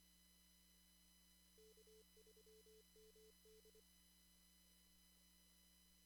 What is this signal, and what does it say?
Morse code, heard as a keyed tone, about 24 wpm, 440 Hz. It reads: K3MD
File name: CW / kt5j wpx cw-op-k5pi